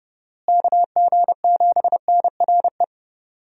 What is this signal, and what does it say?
Morse code, heard as a keyed tone, about 32 wpm, 700 Hz. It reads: KG7NRE
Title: USA Call Signs - 6 Characters - Repeated 3x - 30wpm